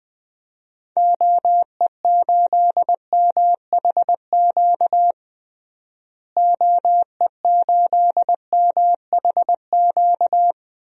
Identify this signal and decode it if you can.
Morse code, heard as a keyed tone, about 20 wpm, 700 Hz. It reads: OE8MHQ OE8MHQ
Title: DXCC-100 Call Signs - 6 Characters - Repeated 2x - 20wpm